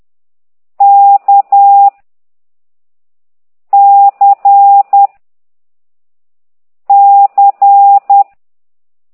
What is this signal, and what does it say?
Morse code, heard as a keyed tone, about 10 wpm, 800 Hz. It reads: K C C
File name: ct2idw / exerc4